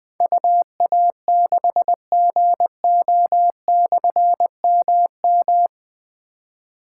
Morse code, heard as a keyed tone, about 20 wpm, 700 Hz. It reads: UA6GO/MM
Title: Difficult Call Signs 20wpm